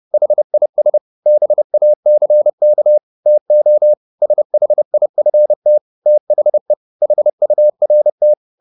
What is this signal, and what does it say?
Morse code, heard as a keyed tone, about 30 wpm, 600 Hz. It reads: HIS BACK TO SHIFT THE HURT
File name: the old man and the sea / the old man and the sea 1 30wpm